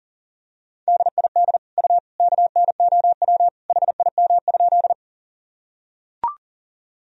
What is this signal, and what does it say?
Morse code, heard as a keyed tone, about 40 wpm, 700 Hz. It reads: DID U KNOW HIM?
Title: Sentences from Top 100 Words and Abbreviations 40wpm